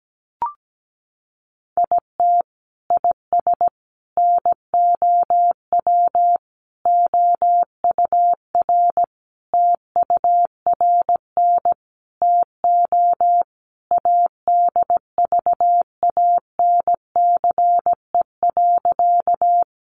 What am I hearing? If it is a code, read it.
Morse code, heard as a keyed tone, about 17 wpm, 700 Hz. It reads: IT IS NOW OUR TURN TO ADVANCE.
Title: Sentences from Top 1400 Words 17wpm